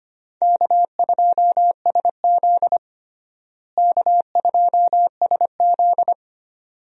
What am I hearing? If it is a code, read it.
Morse code, heard as a keyed tone, about 25 wpm, 700 Hz. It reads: K2SZ K2SZ
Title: USA Call Signs - 4 Characters - Repeated 2x - 25wpm